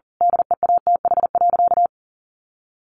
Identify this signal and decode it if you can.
Morse code, heard as a keyed tone, about 40 wpm, 700 Hz. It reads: DEATH.
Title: Sentences from Top 1500 Words 40wpm